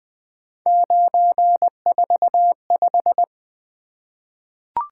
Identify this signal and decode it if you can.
Morse code, heard as a keyed tone, about 20 wpm, 700 Hz. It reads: 945
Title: Three Letter-Number Combinations - 20wpm